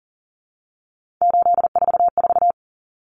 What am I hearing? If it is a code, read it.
Morse code, heard as a keyed tone, about 40 wpm, 700 Hz. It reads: 844
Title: Common Cut Numbers - 3-Digit 40wpm